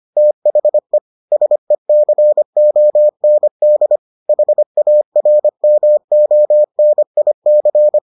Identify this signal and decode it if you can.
Morse code, heard as a keyed tone, about 25 wpm, 600 Hz. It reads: THE SECOND HARMONIC